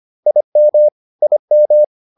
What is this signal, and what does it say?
Morse code, heard as a keyed tone, about 25 wpm, 600 Hz. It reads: IM IM